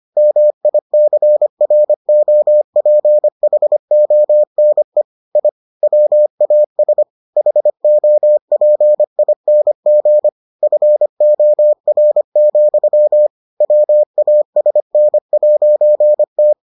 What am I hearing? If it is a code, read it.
Morse code, heard as a keyed tone, about 25 wpm, 600 Hz. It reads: MICROPHONE I WAS HOPING FOR, WASN'T